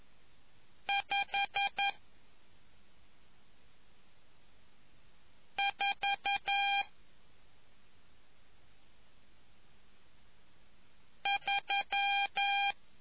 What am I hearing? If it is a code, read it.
Morse code, heard as a keyed tone, about 11 wpm, spaced as for 2 wpm, 800 Hz. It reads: 543